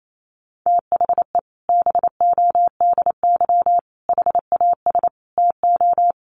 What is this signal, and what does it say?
Morse code, heard as a keyed tone, about 28 wpm, 700 Hz. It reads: THE BODY HAS TO
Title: A Sound Of Thunder 28wpm